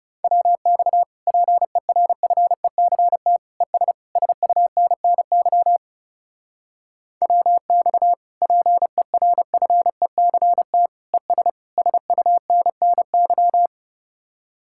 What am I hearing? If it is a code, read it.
Morse code, heard as a keyed tone, about 37 wpm, 700 Hz. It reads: WX PERFECT ES SUNNY WX PERFECT ES SUNNY
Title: Speed-Racing Common QSO Phrases - 30wpm